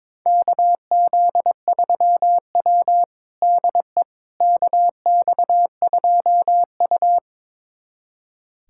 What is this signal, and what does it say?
Morse code, heard as a keyed tone, about 21 wpm, 700 Hz. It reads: KZ3W DE KX2U
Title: USA Callsign DE Callsign - 4 Characters - 22wpm